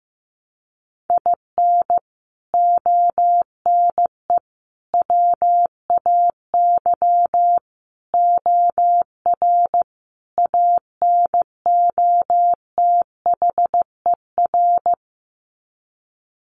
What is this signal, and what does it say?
Morse code, heard as a keyed tone, about 15 wpm, 700 Hz. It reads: IN ONE WAY OR ANOTHER
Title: Sets of 5 Words Encore 15wpm